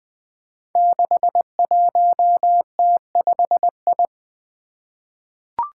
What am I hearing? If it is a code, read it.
Morse code, heard as a keyed tone, about 20 wpm, 700 Hz. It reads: 61T5I
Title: Five Challenging Letter-Number Combinations 20wpm